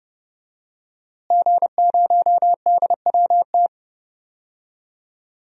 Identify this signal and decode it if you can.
Morse code, heard as a keyed tone, about 31 wpm, 700 Hz. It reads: G0DWT
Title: DXCC-100 Call Signs - 5 Characters - Repeated 2x - 30wpm